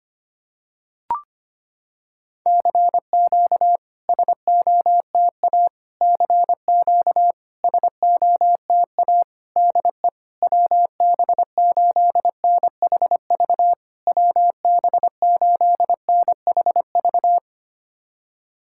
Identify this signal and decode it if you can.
Morse code, heard as a keyed tone, about 25 wpm, 700 Hz. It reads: CQ SOTA CQ SOTA DE WB8NHV WB8NHV
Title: SOTA QSO Practice 25wpm